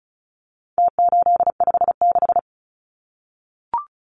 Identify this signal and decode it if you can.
Morse code, heard as a keyed tone, about 34 wpm, 700 Hz. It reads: T856
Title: Numbers - Contest Style - 4-Digit 35wpm